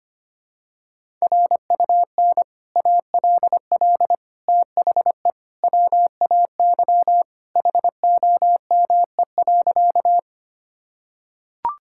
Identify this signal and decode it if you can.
Morse code, heard as a keyed tone, about 25 wpm, 700 Hz. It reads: RUN ALL THE WAY HOME.
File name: Sentences with 3-Word Phrases - 25wpm